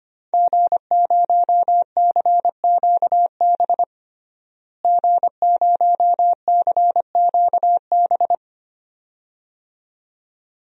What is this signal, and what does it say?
Morse code, heard as a keyed tone, about 25 wpm, 700 Hz. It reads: G0CQB G0CQB
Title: DXCC-100 Call Signs - 5 Characters - Repeated 2x - 25wpm